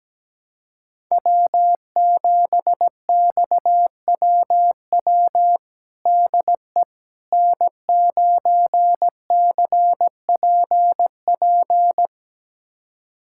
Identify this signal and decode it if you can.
Morse code, heard as a keyed tone, about 17 wpm, 700 Hz. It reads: W7XWW DE N9CPP
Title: USA Callsign DE Callsign - 5 Characters - 17wpm